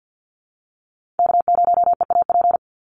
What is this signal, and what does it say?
Morse code, heard as a keyed tone, about 48 wpm, 700 Hz. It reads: K0EAP
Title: NA SOTA Chaser Callsigns 50wpm